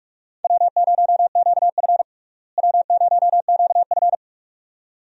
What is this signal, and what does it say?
Morse code, heard as a keyed tone, about 42 wpm, 700 Hz. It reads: W0QF W0QF
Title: USA Call Signs - 4 Characters - Repeated 2x - 45wpm